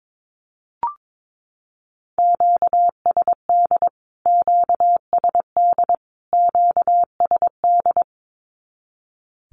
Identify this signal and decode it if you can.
Morse code, heard as a keyed tone, about 22 wpm, 700 Hz. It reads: QSD QSD QSD
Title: Q-Codes Repeated 3x 22wpm